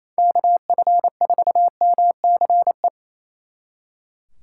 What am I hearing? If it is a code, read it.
Morse code, heard as a keyed tone, about 33 wpm, 700 Hz. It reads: KF4MCE
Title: Speed Racing - USA 2x3 Call Signs 28wpm